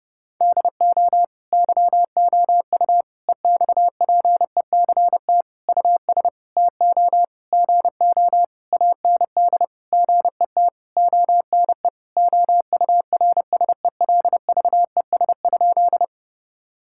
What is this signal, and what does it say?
Morse code, heard as a keyed tone, about 30 wpm, 700 Hz. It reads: DO YOU EXPECT US TO GO AND GET ONE OURSELVES?
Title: Sentences from Top 1600 Words 30wpm